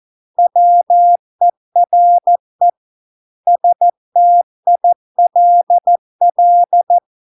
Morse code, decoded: WERE STILL

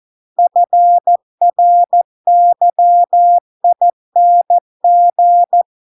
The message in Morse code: FRYING

